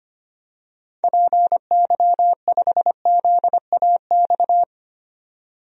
Morse code, heard as a keyed tone, about 25 wpm, 700 Hz. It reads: PY5ZAX
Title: DXCC-100 Call Signs - 6 Characters - 25wpm